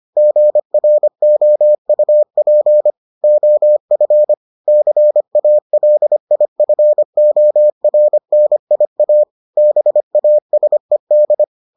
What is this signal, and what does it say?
Morse code, heard as a keyed tone, about 25 wpm, 600 Hz. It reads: GROUP OF CALIFORNIA BASED